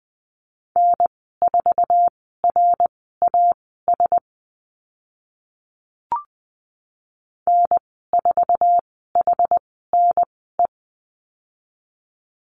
Morse code, heard as a keyed tone, about 20 wpm, spaced as for 14 wpm, 700 Hz. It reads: N4RAS N4HNE
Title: 062 Lesson 11g - Letter h - Call Signs 20wpm